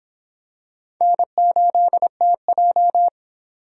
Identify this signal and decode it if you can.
Morse code, heard as a keyed tone, about 26 wpm, 700 Hz. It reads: N8TJ